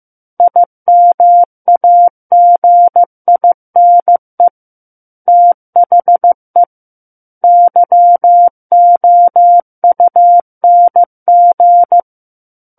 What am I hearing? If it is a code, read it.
Morse code, heard as a keyed tone, about 15 wpm, 700 Hz. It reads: IMAGINE THE YOUNG
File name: Chapter 6 - 15 WPM